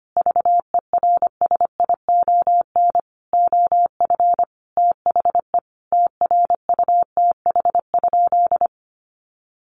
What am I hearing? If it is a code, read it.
Morse code, heard as a keyed tone, about 25 wpm, 700 Hz. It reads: VERSION OF THE TRUTH?
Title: Sentences from Top 1400 Words 25wpm